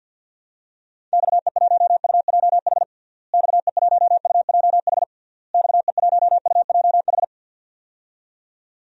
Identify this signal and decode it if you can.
Morse code, heard as a keyed tone, about 46 wpm, 700 Hz. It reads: XE1UJH XE1UJH XE1UJH
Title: DXCC-100 Call Signs - 6 Characters - Repeated 3x - 50wpm